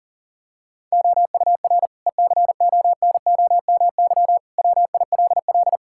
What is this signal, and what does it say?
Morse code, heard as a keyed tone, about 40 wpm, 690 Hz. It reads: OUR ECONOMY WILL
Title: Sentences from Top 1000 Words - Review Part 5 - 40wpm